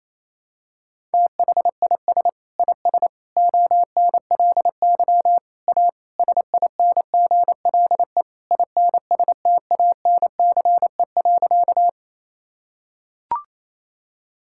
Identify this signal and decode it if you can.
Morse code, heard as a keyed tone, about 28 wpm, 700 Hz. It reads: THIS IS ONLY A SINGLE INSTANCE.
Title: Sentences from Top 1000 Words - Review Part 5 - 28wpm